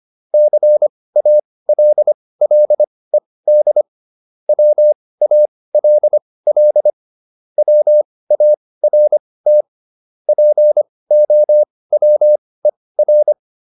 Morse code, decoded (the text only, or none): CALLED WALL WART POWER